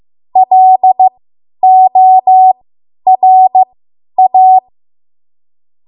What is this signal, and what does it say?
Morse code, heard as a keyed tone, about 15 wpm, spaced as for 10 wpm, 750 Hz. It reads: LORA
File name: Morse / 10WPM